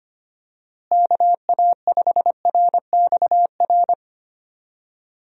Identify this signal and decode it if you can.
Morse code, heard as a keyed tone, about 25 wpm, 700 Hz. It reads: KA5RXR
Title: USA Call Signs - 6 Characters - Repeated 3x - 25wpm